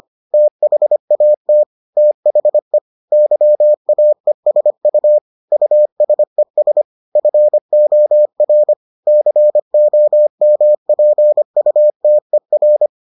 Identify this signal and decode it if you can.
Morse code, heard as a keyed tone, about 25 wpm, 600 Hz. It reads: THAT THE YAESU USES FOR COMPUTER